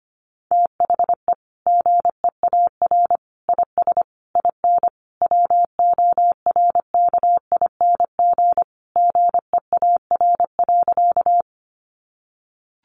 Morse code, heard as a keyed tone, about 25 wpm, 700 Hz. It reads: THE GEAR IS IN WORKING GEAR.
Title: Sentences from Top 1400 Words 25wpm